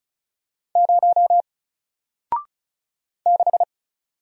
Morse code, heard as a keyed tone, about 36 wpm, 700 Hz. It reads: 0 6
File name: Single-Letter-Number-35wpm